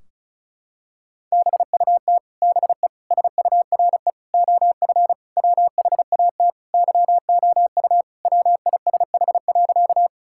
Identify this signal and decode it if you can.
Morse code, heard as a keyed tone, about 34 wpm, 700 Hz. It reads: BUT BE SURE OF WHAT YOU WISH.
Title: Sentences from Top 1600 Words 35wpm